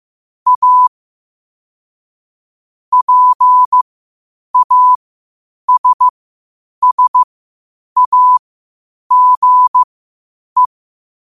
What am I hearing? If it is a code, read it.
Morse code, heard as a keyed tone, about 15 wpm, spaced as for 8 wpm, 1000 Hz. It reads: A PASSAGE
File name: qotdfiles / 8wpm